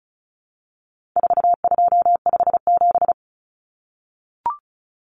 Morse code, heard as a keyed tone, about 34 wpm, 700 Hz. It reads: 4257
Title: Numbers - 4-Digit 35wpm